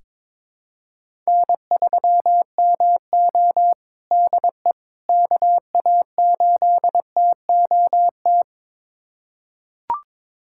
Morse code, heard as a keyed tone, about 22 wpm, 700 Hz. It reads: N3MO DE KA8TOT
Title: USA Callsign DE Callsign 22wpm